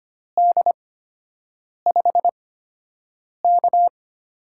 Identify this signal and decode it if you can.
Morse code, heard as a keyed tone, about 25 wpm, spaced as for 7 wpm, 700 Hz. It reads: D5K
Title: USA Call Signs - 4 Characters - 8x Character Spacing - 25wpm